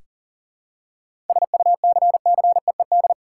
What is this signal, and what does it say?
Morse code, heard as a keyed tone, about 40 wpm, 700 Hz. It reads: SUCCEED
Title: Seven Letter Words - 40wpm